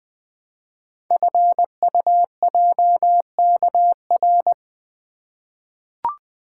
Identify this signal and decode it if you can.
Morse code, heard as a keyed tone, about 20 wpm, 700 Hz. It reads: FUJKR
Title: Code Groups - 5 Letters - 20wpm